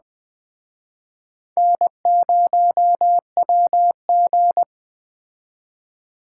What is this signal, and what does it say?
Morse code, heard as a keyed tone, about 21 wpm, 700 Hz. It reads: N0WG